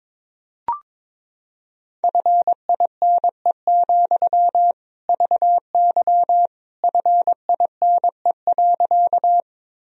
Morse code, heard as a keyed tone, about 22 wpm, 700 Hz. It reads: FINE, VY FINE.
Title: Sentences from Top 500 Words and Abbreviations 22wpm